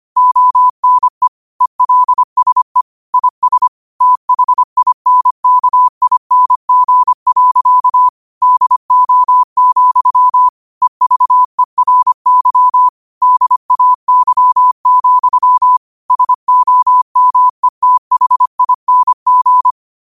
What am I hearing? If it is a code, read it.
Morse code, heard as a keyed tone, about 25 wpm, 1000 Hz. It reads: ONE ELSE IS THINKING. DO, EVERY DAY, SOMETHING